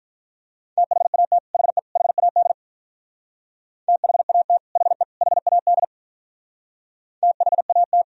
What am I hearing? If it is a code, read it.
Morse code, heard as a keyed tone, about 46 wpm, 700 Hz. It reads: THAT HE HAD THAT HE HAD THAT